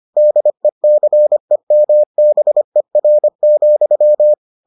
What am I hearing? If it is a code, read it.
Morse code, heard as a keyed tone, about 25 wpm, 600 Hz. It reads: DECEMBER,